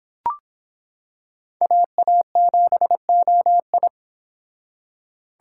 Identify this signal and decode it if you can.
Morse code, heard as a keyed tone, about 26 wpm, 700 Hz. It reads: AA7OI